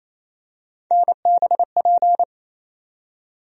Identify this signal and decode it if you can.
Morse code, heard as a keyed tone, about 28 wpm, 700 Hz. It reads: NBP
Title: USA Call Sign Postfixes - 3 Letters 28wpm